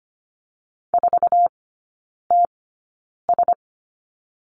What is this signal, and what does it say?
Morse code, heard as a keyed tone, about 25 wpm, spaced as for 9 wpm, 700 Hz. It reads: 4TS
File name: course / 25:10wpm / 100 Lesson 17e - Letter u - Three Characters 25wpm